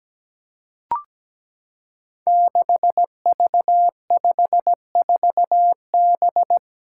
Morse code, heard as a keyed tone, about 17 wpm, 700 Hz. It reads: 6V54B
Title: Five Challenging Letter-Number Combinations 17wpm